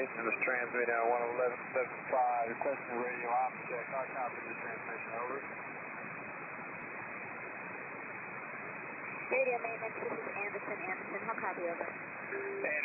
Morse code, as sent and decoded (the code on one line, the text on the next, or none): none